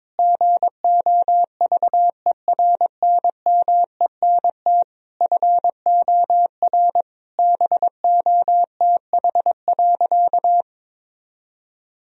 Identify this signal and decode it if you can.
Morse code, heard as a keyed tone, about 22 wpm, 700 Hz. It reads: GOVERNMENT FOR BOTH.
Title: Sentences from Top 1000 Words - Review Part 3 - 22wpm